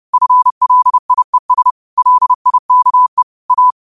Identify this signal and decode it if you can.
Morse code, heard as a keyed tone, about 30 wpm, 1000 Hz. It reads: FLIES LIKE A